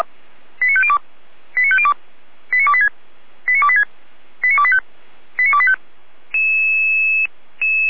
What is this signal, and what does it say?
Five-tone selective call (ZVEI), heard as quick runs of five tones, tones of about 70 ms. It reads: 89502 89502 89286 89286 89285 89285